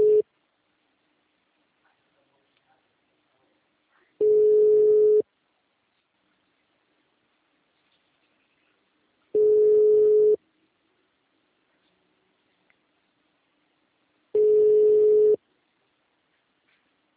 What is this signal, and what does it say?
Ringback tone (Europe, 425 Hz).
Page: Valinhos - SP